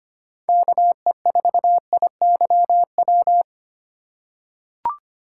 Morse code, heard as a keyed tone, about 25 wpm, 700 Hz. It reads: KE4IYW
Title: USA Call Signs - 6 Characters - 2x Character Spacing - 25wpm